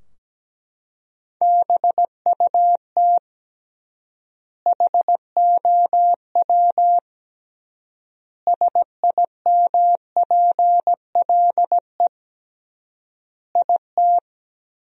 Morse code, heard as a keyed tone, about 17 wpm, 700 Hz. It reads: BUT HOW SIMPLE IT
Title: 3x Word-Spacing - Sentences from Top 400 Words - 17wpm